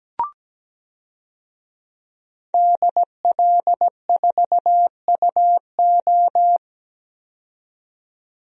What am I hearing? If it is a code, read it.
Morse code, heard as a keyed tone, about 17 wpm, 700 Hz. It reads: DL4UO
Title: DXCC-100 Call Signs - 5 Characters - 17wpm